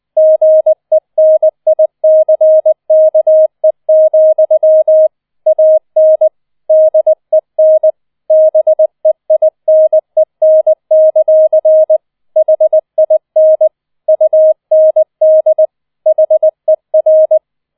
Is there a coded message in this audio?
Yes: Morse code from 0.2 to 17.4 s, heard as a keyed tone, about 20 wpm, 630 Hz. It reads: GENICKE, AN DEN BEINEN; HIN UND HER